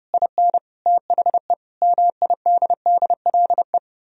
Morse code, heard as a keyed tone, about 38 wpm, 700 Hz. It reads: IN THE MIDDLE